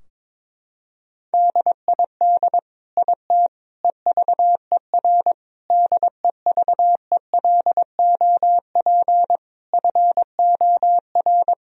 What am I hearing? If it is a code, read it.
Morse code, heard as a keyed tone, about 22 wpm, 700 Hz. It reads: DID IT EVER DEVELOP FOR
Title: Sentences from Top 1000 Words - Review Part 2 - 22wpm